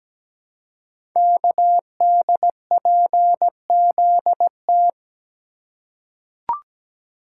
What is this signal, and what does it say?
Morse code, heard as a keyed tone, about 17 wpm, 700 Hz. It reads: KDPZT
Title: Code Groups - 5 Letters - 17wpm